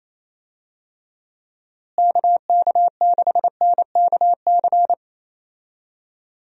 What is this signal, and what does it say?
Morse code, heard as a keyed tone, about 28 wpm, 700 Hz. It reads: KK6NKC
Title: USA Call Signs - 6 Characters - Repeated 3x - 28wpm